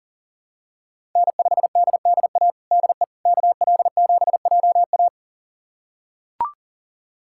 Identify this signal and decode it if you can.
Morse code, heard as a keyed tone, about 37 wpm, 700 Hz. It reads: N5DDA DE KL7JA
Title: USA Callsign DE Callsign - 5 Characters - 40wpm